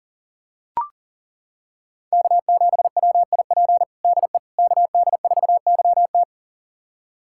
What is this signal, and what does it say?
Morse code, heard as a keyed tone, about 40 wpm, 700 Hz. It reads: K7WIP DE KD4YT
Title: USA Callsign DE Callsign - 5 Characters - 40wpm